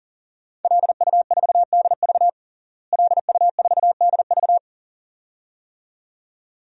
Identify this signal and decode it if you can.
Morse code, heard as a keyed tone, about 37 wpm, 700 Hz. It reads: LU4DV LU4DV
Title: DXCC-100 Call Signs - 5 Characters - Repeated 2x - 40wpm